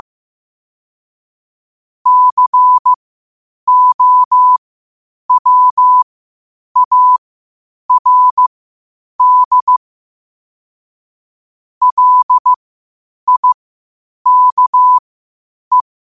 Morse code, heard as a keyed tone, about 15 wpm, spaced as for 8 wpm, 1000 Hz. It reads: COWARD LIKE
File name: qotdfiles / 8wpm